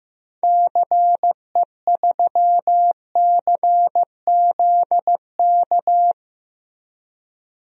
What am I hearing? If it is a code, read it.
Morse code, heard as a keyed tone, about 15 wpm, 700 Hz. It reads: CE3CZK
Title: DXCC-100 Call Signs - 6 Characters - 15wpm